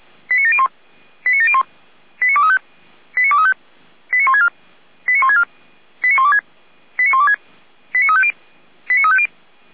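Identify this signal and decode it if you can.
Five-tone selective call (ZVEI), heard as quick runs of five tones, tones of about 70 ms. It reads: 89701 89701 89235 89235 89164 89164 89126 89126 89350 89350
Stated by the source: ZVEI five-tone selective call received on SDR.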